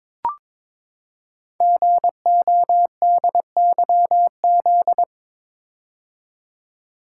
Morse code, heard as a keyed tone, about 22 wpm, 700 Hz. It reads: GODYZ
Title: Code Groups - 5 Letters - 22wpm